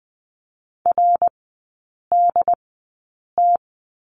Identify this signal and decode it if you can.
Morse code, heard as a keyed tone, about 20 wpm, spaced as for 8 wpm, 700 Hz. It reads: RDT
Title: 100 Lesson 17e - Letter u - Three Characters 20wpm